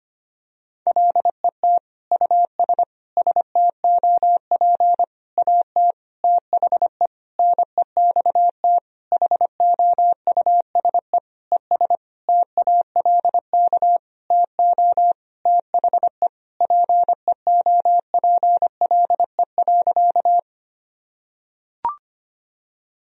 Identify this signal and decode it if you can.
Morse code, heard as a keyed tone, about 25 wpm, 700 Hz. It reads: LET US STOP AT THE NEXT HOUSE ES TALK TO THE PEOPLE.
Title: Sentences from Top 400 Words and Abbreviations 25wpm